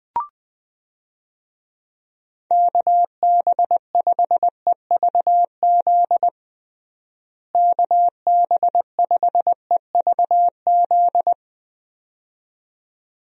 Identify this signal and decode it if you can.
Morse code, heard as a keyed tone, about 20 wpm, 700 Hz. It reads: KB5EVZ KB5EVZ
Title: USA Call Signs - 6 Characters - Repeated 2x - 20wpm